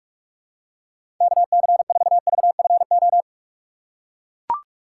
Morse code, heard as a keyed tone, about 41 wpm, 700 Hz. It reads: KC4VFO
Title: USA Call Signs - 6 Characters - 2x Character Spacing - 45wpm